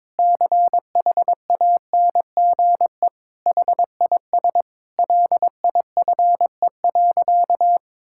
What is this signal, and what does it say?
Morse code, heard as a keyed tone, about 22 wpm, 700 Hz. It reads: CHANGE HIS LIFE.
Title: Sentences from Top 1000 Words - Review Part 5 - 22wpm